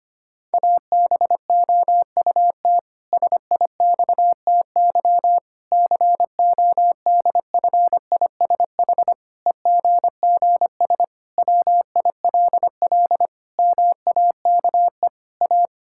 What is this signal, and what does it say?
Morse code, heard as a keyed tone, about 25 wpm, 700 Hz. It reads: ABOUT SIXTY CODFISH EGGS WILL MAKE A
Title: Pangram Challenge v2 - 25wpm